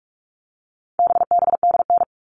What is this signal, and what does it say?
Morse code, heard as a keyed tone, about 44 wpm, 700 Hz. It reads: BBDN
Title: All Cut Numbers - 4-Digit 45wpm